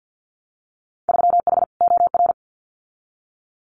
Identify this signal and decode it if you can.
Morse code, heard as a keyed tone, about 50 wpm, 700 Hz. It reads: 3H OR
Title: Winter Field Day Exchanges Encore 50wpm